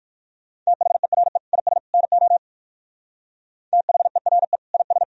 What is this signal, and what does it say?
Morse code, heard as a keyed tone, about 57 wpm, 700 Hz. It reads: THERE IS NO THERE IS